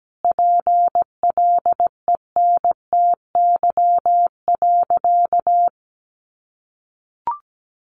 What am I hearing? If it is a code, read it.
Morse code, heard as a keyed tone, about 17 wpm, 700 Hz. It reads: PLENTY.
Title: Sentences from Top 1300 Words 17wpm